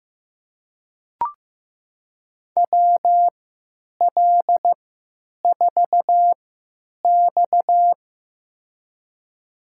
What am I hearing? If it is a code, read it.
Morse code, heard as a keyed tone, about 15 wpm, spaced as for 9 wpm, 700 Hz. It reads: WL4X
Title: USA Call Signs - 4 Characters - 3x Character Spacing - 15wpm